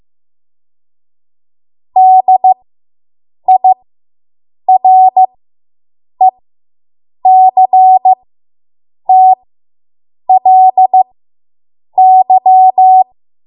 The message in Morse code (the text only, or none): DIRECTLY